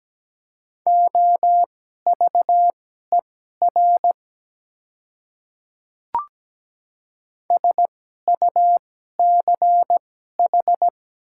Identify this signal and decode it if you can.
Morse code, heard as a keyed tone, about 17 wpm, spaced as for 12 wpm, 690 Hz. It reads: OVER SUCH
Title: 2x Character Spacing - Top 300 Words - 17wpm